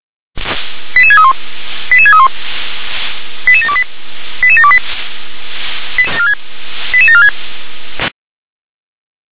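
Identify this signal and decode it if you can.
Five-tone selective call (ZVEI), heard as quick runs of five tones, tones of about 70 ms. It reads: 88631 88631 88728 88728 88746
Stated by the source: ZVEI five-tone selective call 88500, received by SDR.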